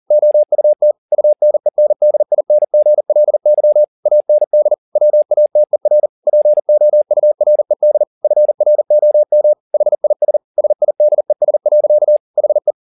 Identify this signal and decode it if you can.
Morse code, heard as a keyed tone, about 40 wpm, 600 Hz. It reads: OUT UNENDINGLY AND WATER POURED FROM HIS SIDES. HE